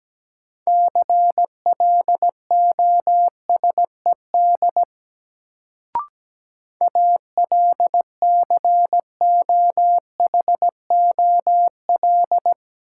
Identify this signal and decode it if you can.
Morse code, heard as a keyed tone, about 17 wpm, 700 Hz. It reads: CLOSED ALCOHOL
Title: Sentences from Top 1400 Words - New Word Drill - 17wpm